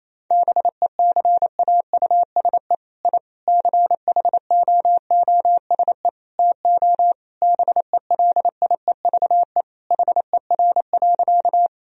Morse code, heard as a keyed tone, about 28 wpm, 700 Hz. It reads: BECAUSE I CHOOSE TO BELIEVE HER.
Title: Sentences from Top 1000 Words - Review Part 4 - 28wpm